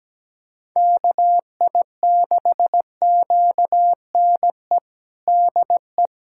Morse code, heard as a keyed tone, about 17 wpm, 700 Hz. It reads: KI6QNE DE
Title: USA Callsign DE Callsign 17wpm